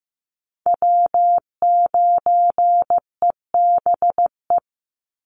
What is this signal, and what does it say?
Morse code, heard as a keyed tone, about 15 wpm, 700 Hz. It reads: W9EBE
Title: Call Signs Repeated 3x - 15wpm